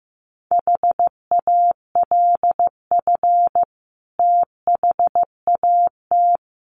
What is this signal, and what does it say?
Morse code, heard as a keyed tone, about 15 wpm, 700 Hz. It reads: HALF THAT
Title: Sentences from Top 1400 Words 15wpm